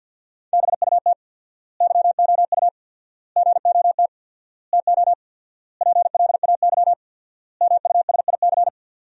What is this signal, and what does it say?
Morse code, heard as a keyed tone, about 50 wpm, 700 Hz. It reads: BUT YOU GOT TO PLAY MUSIC
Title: 4x Word-Spacing - Sentences from Top 300 Words - 50wpm